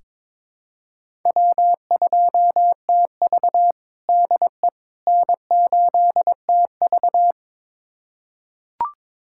Morse code, heard as a keyed tone, about 22 wpm, 700 Hz. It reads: W2TV DE N8TV
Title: USA Callsign DE Callsign - 4 Characters - 22wpm